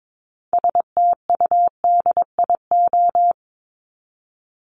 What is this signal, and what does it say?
Morse code, heard as a keyed tone, about 22 wpm, 700 Hz. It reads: STUDIO
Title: Top 1001-1250 Words 22wpm